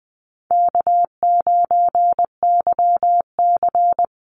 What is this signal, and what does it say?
Morse code, heard as a keyed tone, about 20 wpm, 700 Hz. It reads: K9YC
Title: Easy Call Signs Repeated 3x - 20wpm